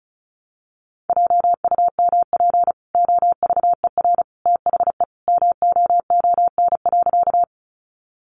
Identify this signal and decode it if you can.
Morse code, heard as a keyed tone, about 34 wpm, 700 Hz. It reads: JUMP OVER THE MOON.
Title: Sentences from Top 1300 Words 35wpm